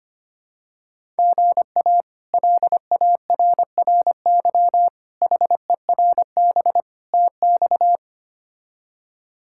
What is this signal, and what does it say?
Morse code, heard as a keyed tone, about 25 wpm, 700 Hz. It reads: GA LARRY HERB TX
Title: K1USN SST QSO Practice - 25wpm